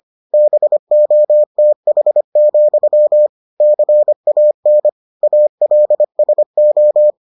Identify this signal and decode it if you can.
Morse code, heard as a keyed tone, about 25 wpm, 600 Hz. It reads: BOTH, CAN ALSO